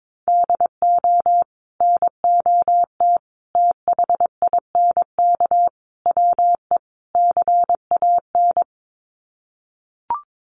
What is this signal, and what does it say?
Morse code, heard as a keyed tone, about 22 wpm, 700 Hz. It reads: DO NOT THINK WE CAN
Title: Sets of 5 Words Encore2 22wpm